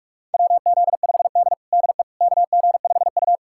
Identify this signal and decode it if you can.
Morse code, heard as a keyed tone, about 42 wpm, 700 Hz. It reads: WZ5D DE KG5U
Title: USA Callsign DE Callsign - 4 Characters - 45wpm